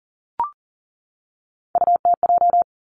Morse code, heard as a keyed tone, about 40 wpm, 700 Hz. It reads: UTJ
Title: Three Challenging Letter-Number Combinations 40wpm